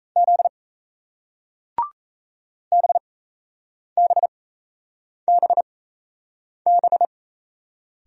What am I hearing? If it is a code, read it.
Morse code, heard as a keyed tone, about 37 wpm, spaced as for 8 wpm, 700 Hz. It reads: Z BBBB